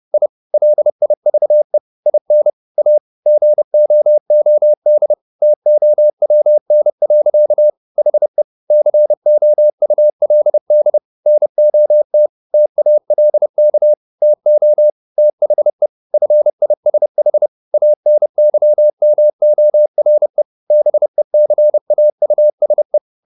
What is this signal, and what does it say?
Morse code, heard as a keyed tone, about 30 wpm, 600 Hz. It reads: I LIVE IN A GOOD TOWN. HE COULD NOT TALK TO THE FISH ANYMORE BECAUSE